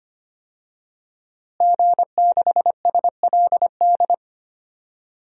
Morse code, heard as a keyed tone, about 25 wpm, 700 Hz. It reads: G6SLD
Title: DXCC-100 Call Signs - 5 Characters - Repeated 3x - 25wpm